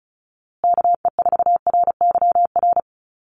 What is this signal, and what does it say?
Morse code, heard as a keyed tone, about 34 wpm, 700 Hz. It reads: KE4RYR
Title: USA Call Signs Repeated 3x 35wpm